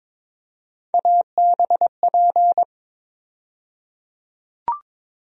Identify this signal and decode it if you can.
Morse code, heard as a keyed tone, about 22 wpm, 700 Hz. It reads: ABP
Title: USA Call Sign Postfixes - 3 Letters 22wpm